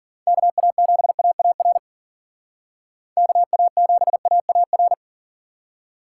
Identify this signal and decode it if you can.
Morse code, heard as a keyed tone, about 47 wpm, 700 Hz. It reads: KA7AAR KA7AAR
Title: Speed Racing - USA 2x3 Call Signs 40wpm